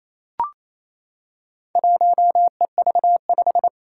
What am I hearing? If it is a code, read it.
Morse code, heard as a keyed tone, about 29 wpm, 700 Hz. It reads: 1EV5
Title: Four Challenging Letter-Number Combinations 28wpm